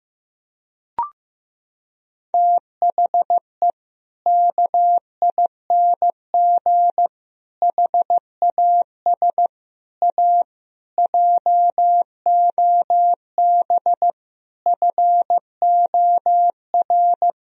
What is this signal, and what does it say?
Morse code, heard as a keyed tone, about 15 wpm, 700 Hz. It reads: THE KING HAS A JOB FOR
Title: Sentences from Top 1000 Words - Review Part 3 - 15wpm